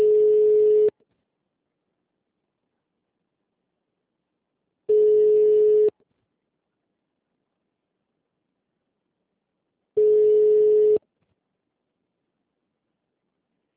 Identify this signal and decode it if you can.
Ringback tone (Europe, 425 Hz).